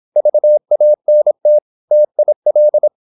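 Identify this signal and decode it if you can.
Morse code, heard as a keyed tone, about 26 wpm, 600 Hz. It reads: VANT TIL
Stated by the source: High-speed (QRQ) Morse at 26 wpm.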